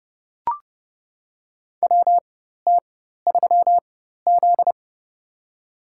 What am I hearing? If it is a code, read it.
Morse code, heard as a keyed tone, about 30 wpm, spaced as for 14 wpm, 700 Hz. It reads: WT3Z
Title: USA Call Signs - 4 Characters - 4x Character Spacing - 30wpm